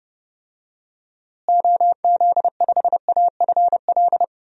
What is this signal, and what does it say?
Morse code, heard as a keyed tone, about 30 wpm, 700 Hz. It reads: OZ5AFL
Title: DXCC-100 Call Signs - 6 Characters - Repeated 3x - 30wpm